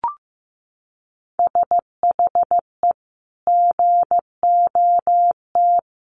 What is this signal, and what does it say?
Morse code, heard as a keyed tone, about 15 wpm, 700 Hz. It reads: SHE GOT
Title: Sentences from Top 1100 Words 15wpm